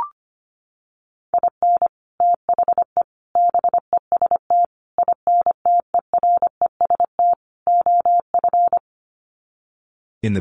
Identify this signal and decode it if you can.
Morse code, heard as a keyed tone, about 25 wpm, 700 Hz. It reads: IN THE BEST INTEREST OF
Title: Sets of 5 Words Encore 25wpm